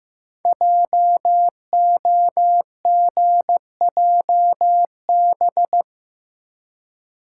Morse code, heard as a keyed tone, about 15 wpm, 700 Hz. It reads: JOGJB
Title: Code Groups - 5 Letters - 15wpm